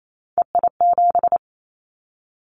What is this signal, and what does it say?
Morse code, heard as a keyed tone, about 28 wpm, 700 Hz. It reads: EI7
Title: Callsign Prefixes for Top 50 DXCC - 28wpm